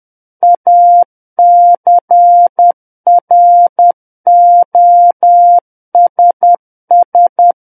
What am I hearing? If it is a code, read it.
Morse code, heard as a keyed tone, about 10 wpm, 700 Hz. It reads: ACROSS
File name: Book One / 01-Chapter 1 - 10 WPM